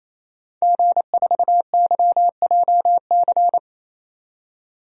Morse code, heard as a keyed tone, about 28 wpm, 700 Hz. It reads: G4YJC
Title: DXCC-100 Call Signs - 5 Characters - 28wpm